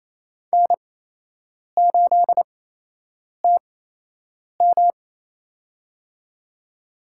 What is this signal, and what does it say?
Morse code, heard as a keyed tone, about 28 wpm, spaced as for 8 wpm, 700 Hz. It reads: N8TM
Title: USA Call Signs - 8x Character Spacing - 28wpm